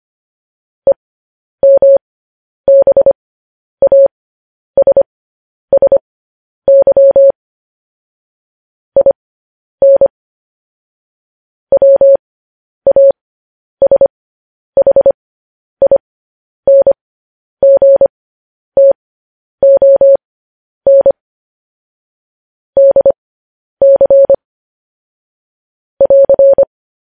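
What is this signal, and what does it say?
Morse code, heard as a keyed tone, about 25 wpm, spaced as for 10 wpm, 550 Hz. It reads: EMBASSY IN WASHINGTON DC <AR>
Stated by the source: News headlines sent in Morse code at 10 wpm.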